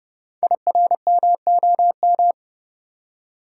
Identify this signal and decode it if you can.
Morse code, heard as a keyed tone, about 29 wpm, 690 Hz. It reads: IRMOM
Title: Code Groups - 5 Letters - 30wpm